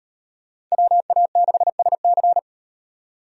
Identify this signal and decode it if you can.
Morse code, heard as a keyed tone, about 38 wpm, 700 Hz. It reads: WA6SC